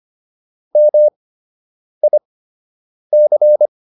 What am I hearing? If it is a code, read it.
Morse code, heard as a keyed tone, about 25 wpm, spaced as for 8 wpm, 600 Hz. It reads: MIC